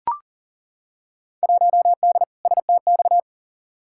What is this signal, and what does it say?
Morse code, heard as a keyed tone, about 40 wpm, 700 Hz. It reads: 1D STX
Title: Field Day Exchanges - 40wpm